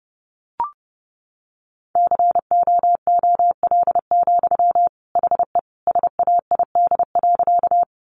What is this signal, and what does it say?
Morse code, heard as a keyed tone, about 30 wpm, 700 Hz. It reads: COOL, HE SAID.
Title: Sentences from Top 1200 Words 30wpm